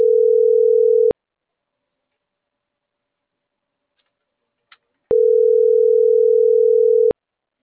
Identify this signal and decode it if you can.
Ringback tone (North America), 440 + 480 Hz.